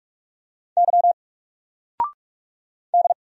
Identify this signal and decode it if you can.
Morse code, heard as a keyed tone, about 44 wpm, 700 Hz. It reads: Y D
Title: Single Character (Letter:Number:Punctuation:Prosign) 45wpm